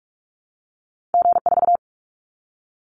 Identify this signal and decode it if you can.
Morse code, heard as a keyed tone, about 44 wpm, 700 Hz. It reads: G4
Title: Callsign Prefixes for Top 25 DXCC - Repeated 3x - 45wpm